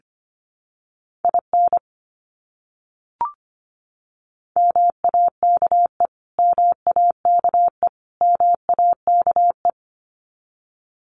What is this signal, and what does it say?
Morse code, heard as a keyed tone, about 25 wpm, 700 Hz. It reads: IN MAKE MAKE MAKE
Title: Top 300 Words Repeated 25wpm